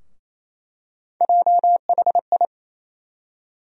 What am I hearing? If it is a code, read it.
Morse code, heard as a keyed tone, about 28 wpm, 700 Hz. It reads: JHI
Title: USA Call Sign Postfixes - 3 Letters 28wpm